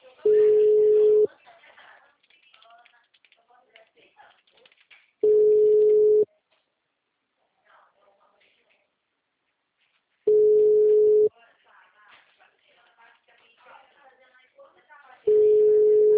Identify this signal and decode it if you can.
Ringback tone (Europe, 425 Hz), with speech.